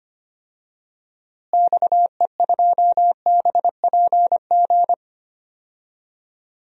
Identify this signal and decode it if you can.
Morse code, heard as a keyed tone, about 25 wpm, 700 Hz. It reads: XE2BPG
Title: DXCC-100 Call Signs - 6 Characters - Repeated 3x - 25wpm